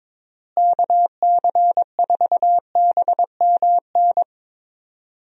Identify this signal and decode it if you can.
Morse code, heard as a keyed tone, about 22 wpm, 700 Hz. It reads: KC4BMN
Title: USA Call Signs - 6 Characters - Repeated 3x - 22wpm